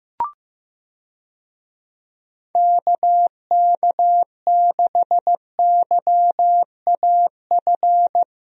Morse code, heard as a keyed tone, about 16 wpm, 700 Hz. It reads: KK6YAF
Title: USA Call Signs - 6 Characters - Repeated 3x - 15wpm